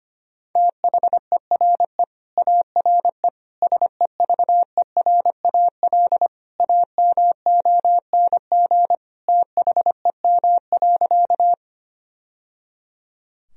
Morse code, heard as a keyed tone, about 25 wpm, 700 Hz. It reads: THERE ARE SEVERAL AMONG THEM.
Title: Sentences from Top 1000 Words - Review Part 2 - 25wpm